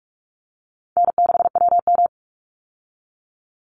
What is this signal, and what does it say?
Morse code, heard as a keyed tone, about 44 wpm, 700 Hz. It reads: N6WM
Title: Easy Call Signs - 45wpm